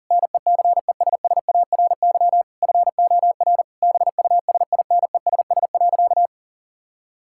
Morse code, heard as a keyed tone, about 40 wpm, 700 Hz. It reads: NECESSARY FOR BUSINESS.
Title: Sentences from Top 1000 Words - Review Part 5 - 40wpm